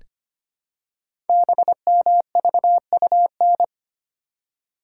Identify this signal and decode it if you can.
Morse code, heard as a keyed tone, about 25 wpm, 700 Hz. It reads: BMVUN
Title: Code Groups - 5 Letters - 25wpm